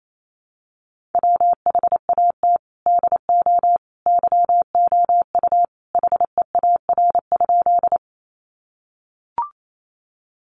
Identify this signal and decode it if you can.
Morse code, heard as a keyed tone, about 28 wpm, 700 Hz. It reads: WHAT DO YOU HEAR?
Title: Sentences from Top 300 Words 28wpm